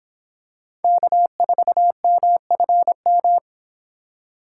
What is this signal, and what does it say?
Morse code, heard as a keyed tone, about 26 wpm, 700 Hz. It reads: K4MFM